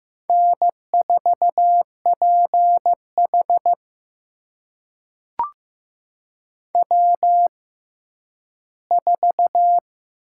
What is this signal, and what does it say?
Morse code, heard as a keyed tone, about 15 wpm, spaced as for 10 wpm, 700 Hz. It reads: N4PH W4
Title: USA Call Signs - 4 Characters - 6x Character Spacing - 15wpm